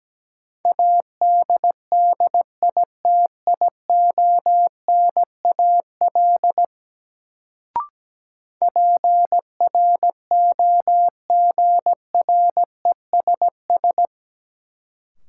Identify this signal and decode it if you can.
Morse code, heard as a keyed tone, about 17 wpm, 700 Hz. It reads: ADDITIONAL PROGRESS
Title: Sentences from Top 1400 Words - New Word Drill Alt - 17wpm